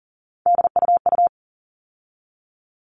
Morse code, heard as a keyed tone, about 40 wpm, 700 Hz. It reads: DUU
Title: All Cut Numbers - 3-Digit 40wpm